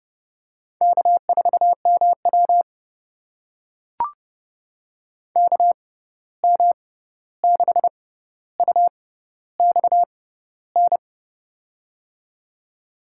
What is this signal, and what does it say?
Morse code, heard as a keyed tone, about 30 wpm, spaced as for 14 wpm, 700 Hz. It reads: K4MW KM6UXN
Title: USA Call Signs - 6x Character Spacing - 30wpm